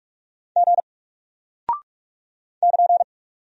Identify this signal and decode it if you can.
Morse code, heard as a keyed tone, about 46 wpm, 700 Hz. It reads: G <KN>
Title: Single Character (Letter:Number:Punctuation:Prosign) Alt 45wpm